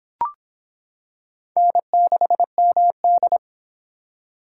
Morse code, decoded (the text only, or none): N6MD